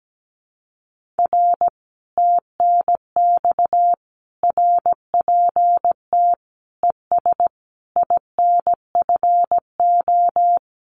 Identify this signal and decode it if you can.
Morse code, heard as a keyed tone, about 17 wpm, 700 Hz. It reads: R TNX RPT ES INFO
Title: Common QSO Phrases Encore 17wpm